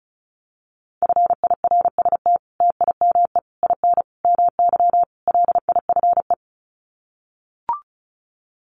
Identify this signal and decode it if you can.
Morse code, heard as a keyed tone, about 34 wpm, 700 Hz. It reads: FIRST TIME IN MY LIFE
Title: Sets of 5 Words Encore2 35wpm